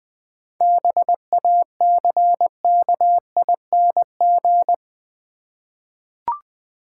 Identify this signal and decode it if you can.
Morse code, heard as a keyed tone, about 20 wpm, 700 Hz. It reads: BACKING
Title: 7 Letter Words - First 3 Letters Given - 20wpm